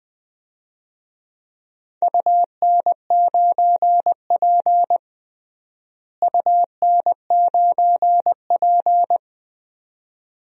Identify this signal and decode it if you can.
Morse code, heard as a keyed tone, about 20 wpm, 700 Hz. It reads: UN9P UN9P
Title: DXCC-100 Call Signs - 4 Characters - Repeated 2x - 20wpm